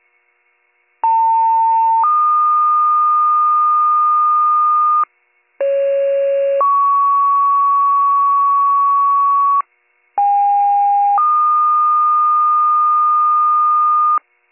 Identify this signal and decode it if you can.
Two-tone page, heard as a one-second tone, then a three-second tone, about 900 then 1220 Hz, about 550 then 1120 Hz, about 810 then 1220 Hz. It is heard from 1.0 to 14.2 s.